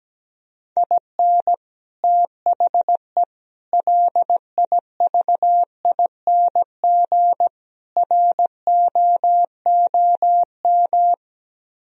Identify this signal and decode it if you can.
Morse code, heard as a keyed tone, about 17 wpm, 700 Hz. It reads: IN THE LIVING ROOM
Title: Speed-Racing - Sets of 4 Words 17wpm